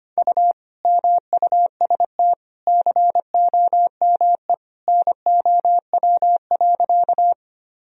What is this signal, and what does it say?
Morse code, heard as a keyed tone, about 25 wpm, 700 Hz. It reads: U MUST COME NOW.
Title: Sentences from Top 200 Words and Abbreviations 25wpm